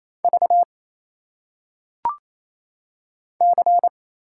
Morse code, heard as a keyed tone, about 27 wpm, 700 Hz. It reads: V C
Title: Single Letters - Phonetic Pronunciation 28wpm